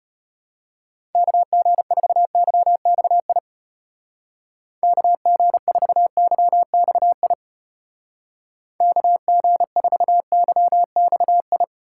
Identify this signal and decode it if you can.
Morse code, heard as a keyed tone, about 31 wpm, 700 Hz. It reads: KG4YXI KG4YXI KG4YXI
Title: Speed Racing - USA 2x3 Call Signs 25wpm